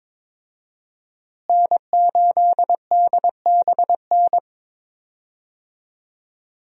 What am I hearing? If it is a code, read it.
Morse code, heard as a keyed tone, about 22 wpm, 700 Hz. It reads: N8DBN
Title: USA Call Signs - 5 Characters - Repeated 2x - 22wpm